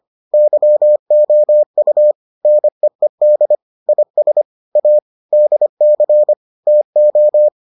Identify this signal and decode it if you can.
Morse code, heard as a keyed tone, about 25 wpm, 600 Hz. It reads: YOU NEED IS A DC TO